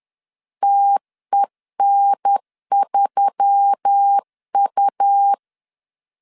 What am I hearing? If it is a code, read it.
Morse code, heard as a keyed tone, about 10 wpm, 800 Hz. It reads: TEN3U
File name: CW / DET / 1 6 1500C S B 10wpm 4137